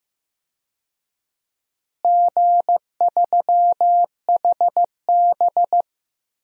Morse code, heard as a keyed tone, about 15 wpm, 700 Hz. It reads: G3HB
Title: DXCC-100 Call Signs - 4 Characters - Repeated 2x - 15wpm